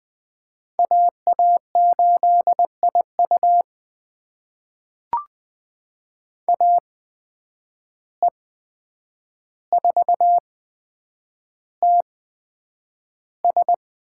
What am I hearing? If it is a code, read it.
Morse code, heard as a keyed tone, about 20 wpm, spaced as for 9 wpm, 700 Hz. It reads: AA8IU AE4TS
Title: USA Call Signs - 5 Characters - 8x Character Spacing - 20wpm